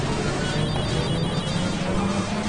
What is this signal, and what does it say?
There is no code.